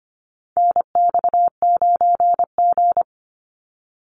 Morse code, heard as a keyed tone, about 25 wpm, 700 Hz. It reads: NX9G
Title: Call Signs Repeated 3x - 25wpm